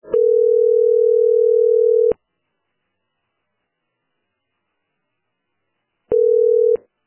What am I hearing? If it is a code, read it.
Ringback tone (North America), 440 + 480 Hz.